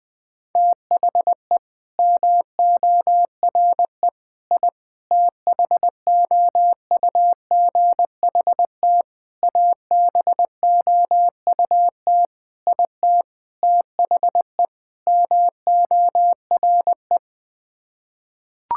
Morse code, heard as a keyed tone, about 20 wpm, 700 Hz. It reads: THE MORE I THOUGHT ABOUT IT THE MORE
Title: Sets of 8 Words - 20wpm